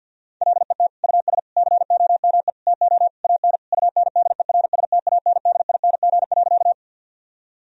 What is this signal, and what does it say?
Morse code, heard as a keyed tone, about 50 wpm, 700 Hz. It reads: LET US COME TO AN UNDERSTANDING.
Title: Sentences from Top 1000 Words - Review Part 3 - 50wpm